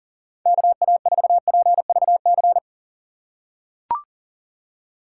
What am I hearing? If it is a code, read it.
Morse code, heard as a keyed tone, about 40 wpm, 700 Hz. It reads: KA4PVC
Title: USA Call Signs - 6 Characters - 2x Character Spacing - 40wpm